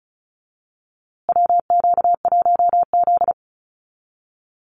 Morse code, heard as a keyed tone, about 34 wpm, 700 Hz. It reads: WQ1Z
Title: USA Call Signs Repeated 2x 35wpm